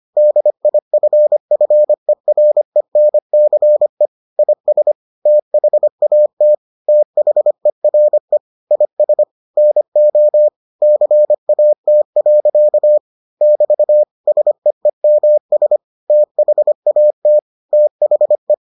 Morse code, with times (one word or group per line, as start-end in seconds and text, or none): 0.2-4.1 DIFFERENCE
4.4-4.9 IS
5.3-6.5 THAT
6.9-8.4 THERE
8.7-9.2 IS
9.6-10.5 NO
10.8-13.0 CAT.
13.4-14.0 <BT>
14.3-15.8 SEEMS
16.1-17.4 THAT
17.7-18.5 THE